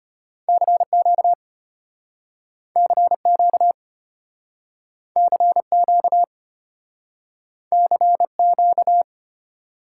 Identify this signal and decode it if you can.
Morse code, heard as a keyed tone, about 34 wpm, 700 Hz. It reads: CQ CQ CQ CQ